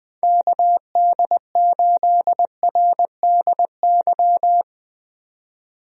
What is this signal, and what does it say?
Morse code, heard as a keyed tone, about 20 wpm, 700 Hz. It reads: KD8RDY
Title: USA Call Signs - 6 Characters - 20wpm